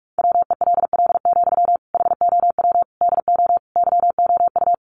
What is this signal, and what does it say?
Morse code, heard as a keyed tone, about 44 wpm, 700 Hz. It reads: WELL, HOW DO YOU
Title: Sentences from Top 1100 Words 45wpm